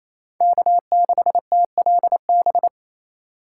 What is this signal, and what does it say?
Morse code, heard as a keyed tone, about 28 wpm, 700 Hz. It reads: K6TLB
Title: USA Call Signs - 5 Characters - 2x Character Spacing - 28wpm